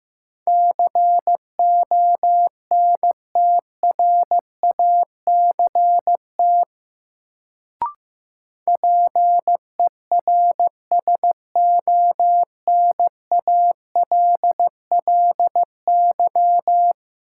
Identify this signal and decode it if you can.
Morse code, heard as a keyed tone, about 16 wpm, 700 Hz. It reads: CONTRACT PERSONALLY
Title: Sentences from Top 1400 Words - New Word Drill Alt - 15wpm